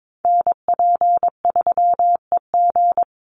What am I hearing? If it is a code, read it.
Morse code, heard as a keyed tone, about 22 wpm, 700 Hz. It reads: NP3EG
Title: USA Call Signs 22wpm